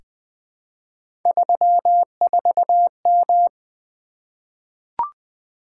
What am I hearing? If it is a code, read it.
Morse code, heard as a keyed tone, about 20 wpm, 700 Hz. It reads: 34M
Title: Three Letter-Number Combinations - 20wpm